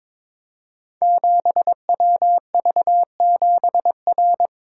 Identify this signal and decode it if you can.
Morse code, heard as a keyed tone, about 22 wpm, 700 Hz. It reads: 7WV7R
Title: Code Groups - 5 Letter:Numbers - 22wpm